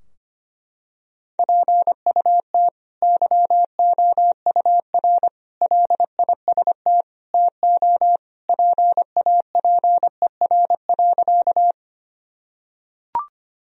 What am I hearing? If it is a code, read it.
Morse code, heard as a keyed tone, about 25 wpm, 700 Hz. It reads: PUT YOUR LIST TO PAPER.
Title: Sentences from Top 1000 Words - Review Part 2 - 25wpm